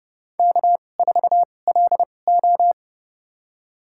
Morse code, heard as a keyed tone, about 31 wpm, spaced as for 22 wpm, 700 Hz. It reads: K4LO
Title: USA Call Signs - 4 Characters - 2x Character Spacing - 30wpm